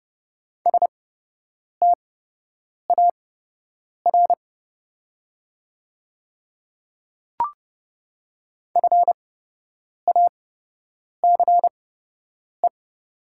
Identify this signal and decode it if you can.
Morse code, heard as a keyed tone, about 30 wpm, spaced as for 8 wpm, 700 Hz. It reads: STAR FACE
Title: 8x Character Spacing - Top 500 Words - 30wpm